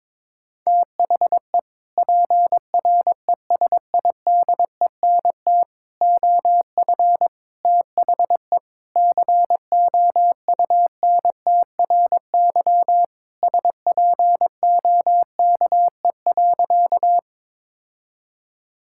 Morse code, heard as a keyed tone, about 22 wpm, 700 Hz. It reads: THE PRESIDENT OF THE COUNTRY SPOKE.
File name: practice / 22wpm / Sentences with 3-Word Phrases - 22wpm